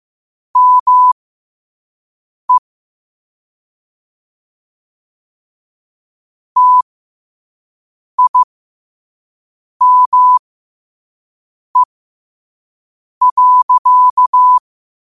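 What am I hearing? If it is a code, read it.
Morse code, heard as a keyed tone, about 15 wpm, spaced as for 5 wpm, 1000 Hz. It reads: ME TIME.
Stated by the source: Morse sent at 5 wpm.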